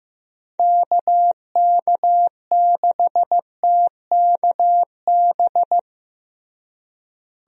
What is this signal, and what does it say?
Morse code, heard as a keyed tone, about 15 wpm, 700 Hz. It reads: KK6TKB
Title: USA Call Signs - 6 Characters - Repeated 3x - 15wpm